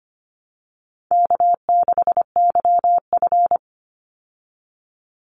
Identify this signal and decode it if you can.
Morse code, heard as a keyed tone, about 25 wpm, 700 Hz. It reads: K6YF
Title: Easy Call Signs Repeated 3x - 25wpm